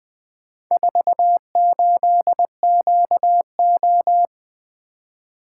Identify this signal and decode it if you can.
Morse code, heard as a keyed tone, about 20 wpm, 700 Hz. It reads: 48QO
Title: Four Letter-Number Combinations - 20wpm